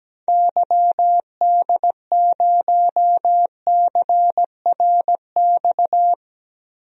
Morse code, heard as a keyed tone, about 17 wpm, 700 Hz. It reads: YD0CRX
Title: DXCC-100 Call Signs - 6 Characters - 17wpm